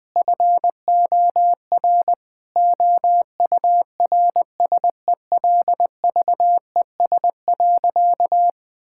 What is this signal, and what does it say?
Morse code, heard as a keyed tone, about 20 wpm, 700 Hz. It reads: FOR OURSELVES.
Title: Sentences from Top 1600 Words 20wpm